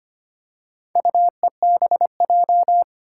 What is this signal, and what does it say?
Morse code, heard as a keyed tone, about 25 wpm, 700 Hz. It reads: UEBJ
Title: Four Challenging Letter-Number Combinations 25wpm